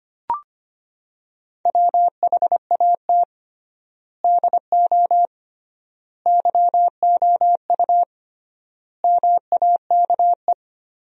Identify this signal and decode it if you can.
Morse code, heard as a keyed tone, about 25 wpm, 700 Hz. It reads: WHAT DO YOU MAKE
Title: 3x Word-Spacing - Sentences from Top 100 Words - 25wpm